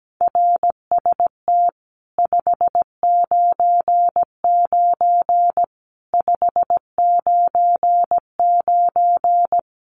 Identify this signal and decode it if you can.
Morse code, heard as a keyed tone, about 17 wpm, 700 Hz. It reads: RST 599 599
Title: Common QSO Phrases 17wpm